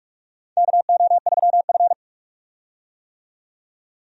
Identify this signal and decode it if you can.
Morse code, heard as a keyed tone, about 46 wpm, 700 Hz. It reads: KO3F
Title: USA Call Signs - 4 Characters - 45wpm